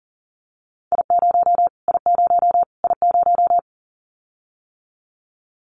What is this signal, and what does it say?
Morse code, heard as a keyed tone, about 40 wpm, 700 Hz. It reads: I0 I0 I0
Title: Callsign Prefixes for Top 25 DXCC - Repeated 3x - 40wpm